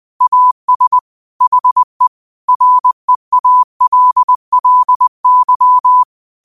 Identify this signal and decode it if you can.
Morse code, heard as a keyed tone, about 20 wpm, 1000 Hz. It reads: AS HE REALLY
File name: qotdfiles / 20wpm